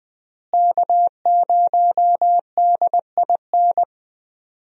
Morse code, heard as a keyed tone, about 20 wpm, 700 Hz. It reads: K0DIN
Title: USA Call Signs - 5 Characters - Repeated 3x - 20wpm